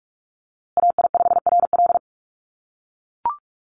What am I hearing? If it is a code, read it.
Morse code, heard as a keyed tone, about 44 wpm, 700 Hz. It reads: AI5RL
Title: USA Call Signs Repeated 2x 45wpm